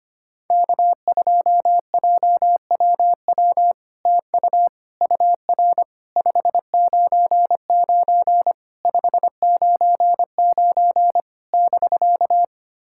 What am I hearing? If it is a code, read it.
Morse code, heard as a keyed tone, about 25 wpm, 700 Hz. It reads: K2JWW TU UR 599 599 <BK>
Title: SOTA QSO Practice 25wpm